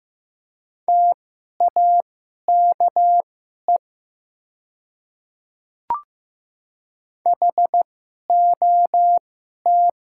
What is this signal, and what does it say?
Morse code, heard as a keyed tone, about 15 wpm, spaced as for 11 wpm, 700 Hz. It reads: TAKE HOT
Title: 2x Character Spacing - Top 200 Words - 15wpm